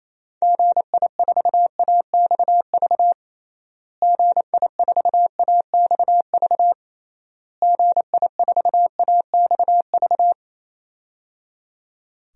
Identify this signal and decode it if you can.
Morse code, heard as a keyed tone, about 28 wpm, 700 Hz. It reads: GI4AXV GI4AXV GI4AXV
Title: DXCC-100 Call Signs - 6 Characters - Repeated 3x - 28wpm